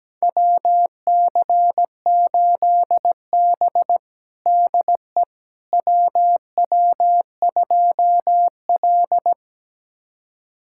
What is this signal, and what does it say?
Morse code, heard as a keyed tone, about 17 wpm, 700 Hz. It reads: WC8B DE WW2L
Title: USA Callsign DE Callsign - 4 Characters - 17wpm